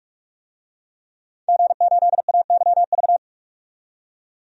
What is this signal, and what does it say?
Morse code, heard as a keyed tone, about 46 wpm, 700 Hz. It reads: G8AYV
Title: DXCC-100 Call Signs - 5 Characters - 45wpm